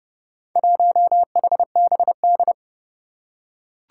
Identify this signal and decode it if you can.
Morse code, heard as a keyed tone, about 30 wpm, 700 Hz. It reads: 1HBD
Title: Four Challenging Letter-Number Combinations 30wpm